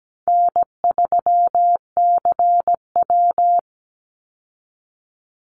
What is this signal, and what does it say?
Morse code, heard as a keyed tone, about 17 wpm, 700 Hz. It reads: N3CW
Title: Easy Call Signs Repeated 3x - 17wpm